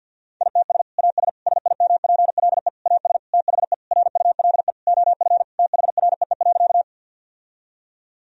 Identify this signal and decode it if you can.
Morse code, heard as a keyed tone, about 50 wpm, 700 Hz. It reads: ITS AS SIMPLE AS THE RULE OF THREE.
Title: Sentences from Top 1000 Words - Review Part 2 - 50wpm